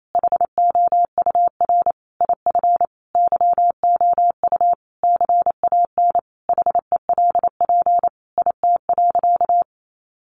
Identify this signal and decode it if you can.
Morse code, heard as a keyed tone, about 28 wpm, 700 Hz. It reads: HOUR IF YOU CAN HELP IT.
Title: Sentences from Top 1100 Words 28wpm